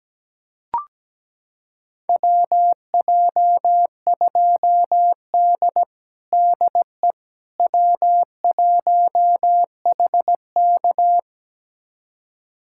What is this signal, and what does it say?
Morse code, heard as a keyed tone, about 17 wpm, 700 Hz. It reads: WJ2D DE W1HK
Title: USA Callsign DE Callsign - 4 Characters - 17wpm